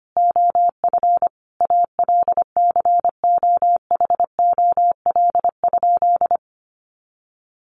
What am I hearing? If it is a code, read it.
Morse code, heard as a keyed tone, about 25 wpm, 700 Hz. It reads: OF ALCOHOL?
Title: Sentences from Top 1400 Words 25wpm